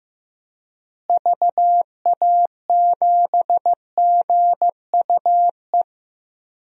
Morse code, heard as a keyed tone, about 15 wpm, 700 Hz. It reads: VA7GUE
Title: DXCC-100 Call Signs - 6 Characters - Repeated 3x - 15wpm